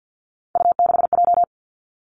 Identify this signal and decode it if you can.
Morse code, heard as a keyed tone, about 50 wpm, 700 Hz. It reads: U6J
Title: Three Challenging Letter-Number Combinations 50wpm